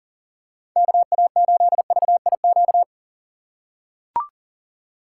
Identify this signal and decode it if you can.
Morse code, heard as a keyed tone, about 40 wpm, 700 Hz. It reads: KA8VIQ
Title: USA Call Signs - 6 Characters - 6x Character Spacing - 40wpm